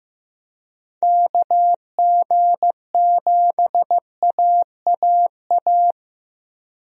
Morse code, heard as a keyed tone, about 16 wpm, 700 Hz. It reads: KG7AAA
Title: USA Call Signs - 6 Characters - 15wpm